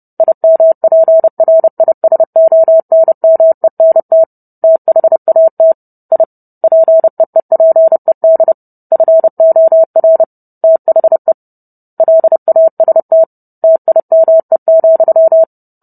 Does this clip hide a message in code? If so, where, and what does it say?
Yes: Morse code from 0.2 to 15.4 s, heard as a keyed tone, about 30 wpm, 650 Hz. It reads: IMPRISONMENT THAT I PEEPED FOR THE LAST TIME,